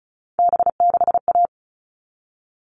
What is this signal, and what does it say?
Morse code, heard as a keyed tone, about 34 wpm, 700 Hz. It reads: B6A
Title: All Cut Numbers - 3-Digit 35wpm